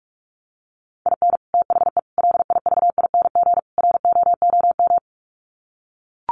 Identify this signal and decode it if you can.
Morse code, heard as a keyed tone, about 44 wpm, 700 Hz. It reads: IN THE LIVING ROOM
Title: Sets of 4 Words 45wpm